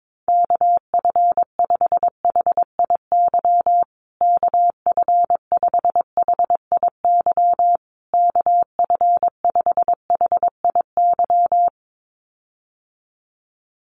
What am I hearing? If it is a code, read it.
Morse code, heard as a keyed tone, about 22 wpm, 700 Hz. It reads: KF5HIY KF5HIY KF5HIY
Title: USA Call Signs Repeated 3x 22wpm